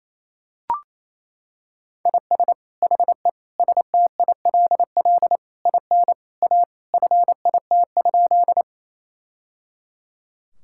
Morse code, heard as a keyed tone, about 28 wpm, 700 Hz. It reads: IS HE STILL IN A FIT?
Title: Sentences from Top 1000 Words - Review Part 5 - 28wpm